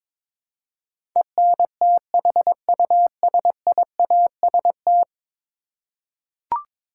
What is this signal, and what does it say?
Morse code, heard as a keyed tone, about 22 wpm, 700 Hz. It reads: ENTHUSIAST
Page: morse-code-ninja-practice-22wpm